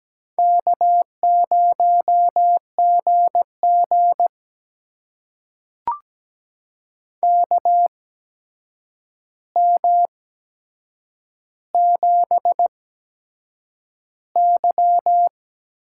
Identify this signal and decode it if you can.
Morse code, heard as a keyed tone, about 17 wpm, spaced as for 7 wpm, 700 Hz. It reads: K0GG KM7Y
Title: USA Call Signs - 4 Characters - 8x Character Spacing - 17wpm